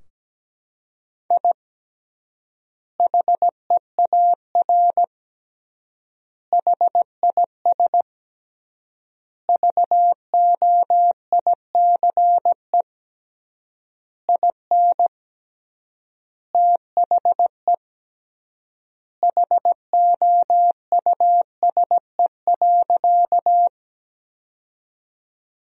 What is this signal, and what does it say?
Morse code, heard as a keyed tone, about 17 wpm, 700 Hz. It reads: I HEAR HIS VOICE IN THE HOUSE.
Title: 3x Word-Spacing - Sentences from Top 500 Words - 17wpm